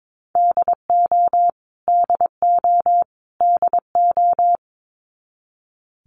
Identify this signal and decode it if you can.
Morse code, heard as a keyed tone, about 22 wpm, 700 Hz. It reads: DO DO DO
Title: Top 200 Words Repeated 22wpm